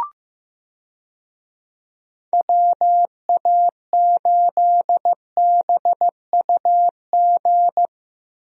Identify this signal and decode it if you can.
Morse code, heard as a keyed tone, about 16 wpm, 700 Hz. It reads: WA8BUG
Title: USA Call Signs - 6 Characters - Repeated 2x - 15wpm